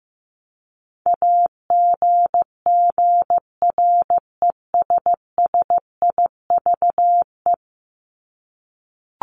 Morse code, heard as a keyed tone, about 15 wpm, 700 Hz. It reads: AGGRESSIVE
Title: Top 1501-1750 Words 15wpm